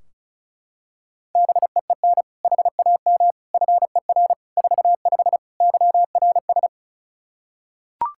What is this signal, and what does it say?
Morse code, heard as a keyed tone, about 36 wpm, 700 Hz. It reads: BEEN HAM FER 45 YRS
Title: K1USN SST QSO Practice 35wpm